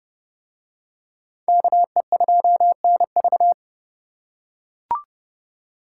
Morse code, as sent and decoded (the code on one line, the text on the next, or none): -.- . ..--- -. ...-
KE2NV